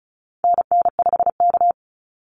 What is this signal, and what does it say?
Morse code, heard as a keyed tone, about 34 wpm, 700 Hz. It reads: NN5K
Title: NA SOTA Activator Callsigns 35wpm